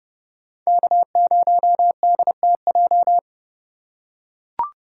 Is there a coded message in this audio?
Yes: Morse code from 0.7 to 3.2 s, heard as a keyed tone, about 30 wpm, 700 Hz. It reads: K0DTJ